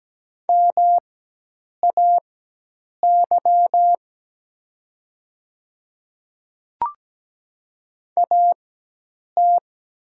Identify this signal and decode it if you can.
Morse code, heard as a keyed tone, about 17 wpm, spaced as for 8 wpm, 700 Hz. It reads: MAY AT
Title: 4x Character Spacing - Top 500 Words - 17wpm